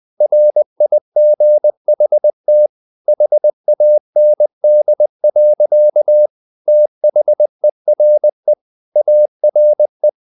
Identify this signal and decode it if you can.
Morse code, heard as a keyed tone, about 20 wpm, 600 Hz. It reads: RIGHT HAND. THERE ARE
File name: the old man and the sea / the old man and the sea part 1 20wpm